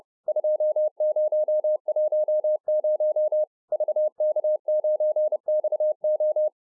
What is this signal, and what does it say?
Morse code, heard as a keyed tone, about 30 wpm, 610 Hz. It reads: 2010 VK9XO